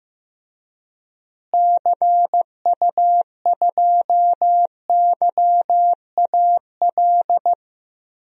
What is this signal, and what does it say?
Morse code, heard as a keyed tone, about 16 wpm, 700 Hz. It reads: CU2YAL
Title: DXCC-100 Call Signs - 6 Characters - Repeated 2x - 15wpm